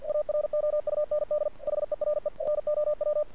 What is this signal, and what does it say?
Morse code, heard as a keyed tone, about 47 wpm, 600 Hz. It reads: AROUND HERE NOW
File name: archived / CRSnet